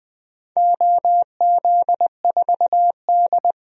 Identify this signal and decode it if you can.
Morse code, heard as a keyed tone, about 21 wpm, 700 Hz. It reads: OZ4D
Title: DXCC-100 Call Signs - 4 Characters - 20wpm